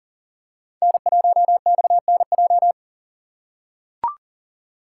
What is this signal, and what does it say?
Morse code, heard as a keyed tone, about 40 wpm, 700 Hz. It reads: N1XNJ
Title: USA Call Signs - 5 Characters - 3x Character Spacing - 40wpm